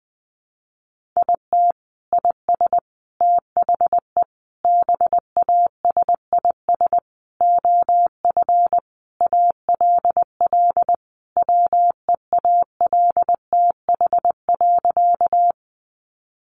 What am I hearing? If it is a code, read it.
Morse code, heard as a keyed tone, about 20 wpm, 700 Hz. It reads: IT IS THE BASIS OF ALL WEALTH.
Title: Sentences from Top 1200 Words 20wpm